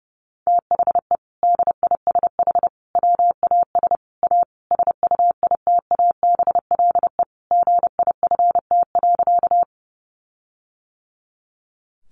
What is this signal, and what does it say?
Morse code, heard as a keyed tone, about 30 wpm, 700 Hz. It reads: THE DISH WAS A SUITABLE GIFT.
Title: Sentences from Top 1400 Words 30wpm